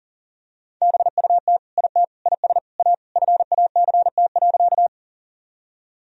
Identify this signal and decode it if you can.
Morse code, heard as a keyed tone, about 40 wpm, 700 Hz. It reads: BUT IT IS A FACT.
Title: Sentences from Top 1000 Words - Review Part 3 - 40wpm